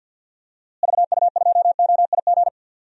Morse code, heard as a keyed tone, about 46 wpm, 700 Hz. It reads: VU2OIG
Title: DXCC-100 Call Signs - 6 Characters - Repeated 2x - 50wpm